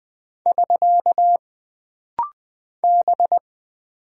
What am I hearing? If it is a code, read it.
Morse code, heard as a keyed tone, about 20 wpm, 700 Hz. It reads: <SK> B
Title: Single Character (Letter:Number:Punctuation:Prosign) - Rapid-Fire - 20wpm